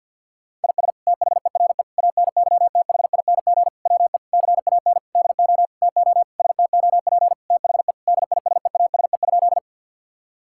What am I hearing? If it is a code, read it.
Morse code, heard as a keyed tone, about 50 wpm, 700 Hz. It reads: IS THERE ANYTHING WE CAN DO TO STOP THE DISEASE?
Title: Sentences from Top 1000 Words - Review Part 6 - 50wpm